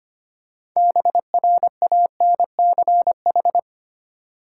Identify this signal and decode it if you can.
Morse code, heard as a keyed tone, about 25 wpm, 700 Hz. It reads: BRANCH
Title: Six Letter Words - 25wpm